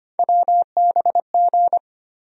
Morse code, heard as a keyed tone, about 25 wpm, 700 Hz. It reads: WBG
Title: USA Call Sign Postfixes - 3 Letters 25wpm